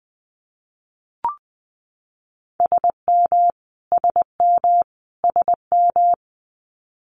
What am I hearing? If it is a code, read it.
Morse code, heard as a keyed tone, about 20 wpm, 700 Hz. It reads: SM SM SM
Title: Multi-letter Phonemes Repeated 3x 20wpm